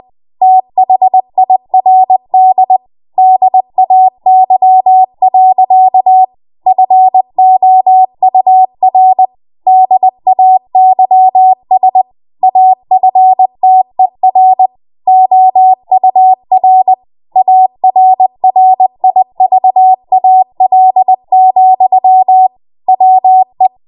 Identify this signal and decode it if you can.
Morse code, heard as a keyed tone, about 20 wpm, 750 Hz. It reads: THIRD DAY. FOUR DAYS AFTER OUR ARRIVAL, WE
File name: Morse / 20WPM